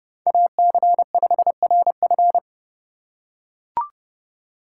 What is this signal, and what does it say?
Morse code, heard as a keyed tone, about 30 wpm, 700 Hz. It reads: AC5RF